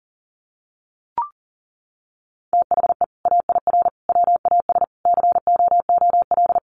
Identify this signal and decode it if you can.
Morse code, heard as a keyed tone, about 40 wpm, 700 Hz. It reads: THE AIR WAS COOL
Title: Sentences from Top 1200 Words 40wpm